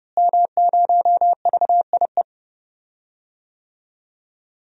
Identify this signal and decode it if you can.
Morse code, heard as a keyed tone, about 30 wpm, 700 Hz. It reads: M0VIE
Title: DXCC-100 Call Signs - 5 Characters - Repeated 2x - 30wpm